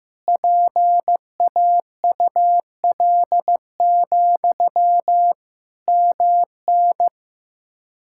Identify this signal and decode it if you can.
Morse code, heard as a keyed tone, about 16 wpm, 700 Hz. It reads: PAUL, MN
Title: K1USN SST QSO Practice 15wpm